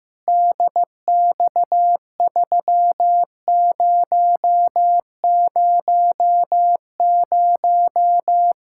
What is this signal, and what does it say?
Morse code, heard as a keyed tone, about 15 wpm, 700 Hz. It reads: DX3000
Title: K1USN SST QSO Practice 15wpm